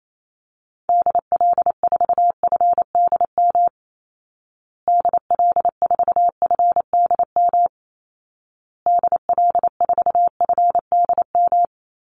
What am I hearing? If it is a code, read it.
Morse code, heard as a keyed tone, about 28 wpm, 700 Hz. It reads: DL4FDM DL4FDM DL4FDM
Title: Call Signs Repeated 2x - 28wpm